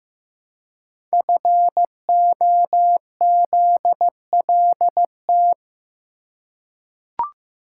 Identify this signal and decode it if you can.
Morse code, heard as a keyed tone, about 15 wpm, 700 Hz. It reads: FOZLT
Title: Code Groups - 5 Letters - 15wpm